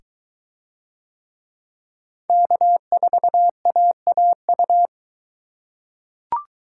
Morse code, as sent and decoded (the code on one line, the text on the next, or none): -.- ....- .- .- ..-
K4AAU